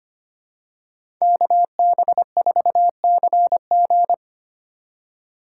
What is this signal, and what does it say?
Morse code, heard as a keyed tone, about 25 wpm, 700 Hz. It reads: KB4CG
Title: K1USN SST QSO Practice - 25wpm